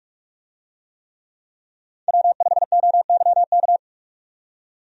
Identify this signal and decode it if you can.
Morse code, heard as a keyed tone, about 44 wpm, 700 Hz. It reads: W5OYK
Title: USA Call Signs - 5 Characters - 45wpm